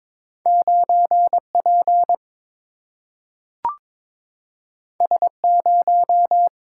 Morse code, heard as a keyed tone, about 22 wpm, 700 Hz. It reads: 9 P S 0
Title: Single-Letter Single-Number 22wpm